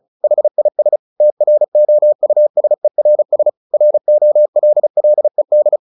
Morse code, heard as a keyed tone, about 34 wpm, 600 Hz. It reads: HIS TROUSERS ROLLED
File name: the old man and the sea 2 35wpm